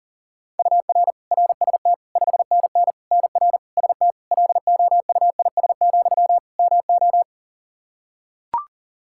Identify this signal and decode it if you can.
Morse code, heard as a keyed tone, about 37 wpm, 700 Hz. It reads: UR RST 5NN NR ST LOUIS, MO
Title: Practice QSO - Exchange 1 - Beginning - 40wpm